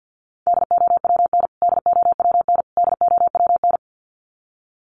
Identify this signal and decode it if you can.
Morse code, heard as a keyed tone, about 50 wpm, 700 Hz. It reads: DOWN DOWN DOWN
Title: Top 500 Words Repeated 50wpm